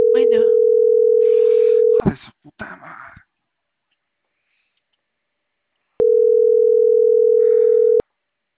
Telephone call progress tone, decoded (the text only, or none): ringback tone